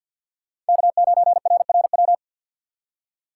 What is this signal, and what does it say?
Morse code, heard as a keyed tone, about 50 wpm, 700 Hz. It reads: K9RRW
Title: USA Call Signs - 2x Character Spacing - 50wpm